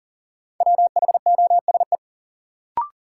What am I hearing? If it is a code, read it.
Morse code, heard as a keyed tone, about 37 wpm, 700 Hz. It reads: WHOSE
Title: Sentences from Top 1200 Words - New Word Drill - 40wpm